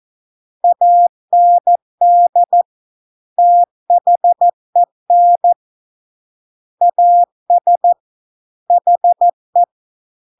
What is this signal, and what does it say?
Morse code, heard as a keyed tone, about 14 wpm, 700 Hz. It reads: AND THEN AS HE